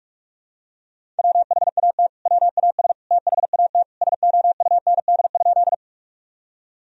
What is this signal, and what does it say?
Morse code, heard as a keyed tone, about 46 wpm, 700 Hz. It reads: WHAT WAS THAT SOUND?
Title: Sentences from Top 1000 Words - Review Part 1 - 45wpm